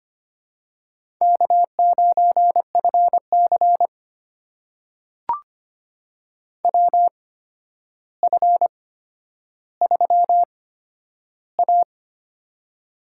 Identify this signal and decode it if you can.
Morse code, heard as a keyed tone, about 25 wpm, spaced as for 11 wpm, 700 Hz. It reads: K9FC WF3A
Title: USA Call Signs - 4 Characters - 8x Character Spacing - 25wpm